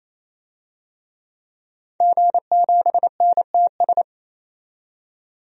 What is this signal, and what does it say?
Morse code, heard as a keyed tone, about 28 wpm, 700 Hz. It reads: G7NTS
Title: DXCC-100 Call Signs - 5 Characters - 28wpm